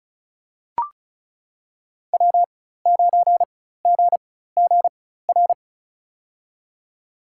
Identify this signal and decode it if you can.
Morse code, heard as a keyed tone, about 36 wpm, spaced as for 16 wpm, 700 Hz. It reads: W9GGR
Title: USA Call Signs - 5 Characters - 4x Character Spacing - 35wpm